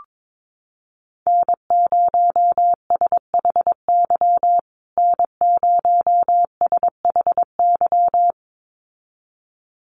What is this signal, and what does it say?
Morse code, heard as a keyed tone, about 22 wpm, 700 Hz. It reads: N0SHY N0SHY
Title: USA Call Signs Repeated 2x 22wpm